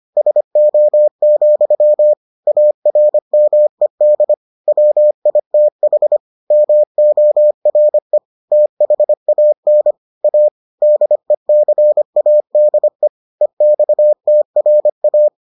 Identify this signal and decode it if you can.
Morse code, heard as a keyed tone, about 25 wpm, 600 Hz. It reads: SO, ARMED WITH MORE THAN A DECADE EXTRA